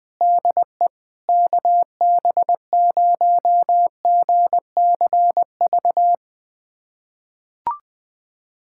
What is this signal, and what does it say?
Morse code, heard as a keyed tone, about 20 wpm, 700 Hz. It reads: DE KB0GCV
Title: USA Callsign DE Callsign - 6 Characters - 20wpm